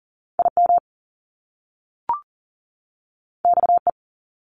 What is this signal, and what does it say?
Morse code, heard as a keyed tone, about 38 wpm, 700 Hz. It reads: IM XE
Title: Top Two Letter Combinations in Words 40wpm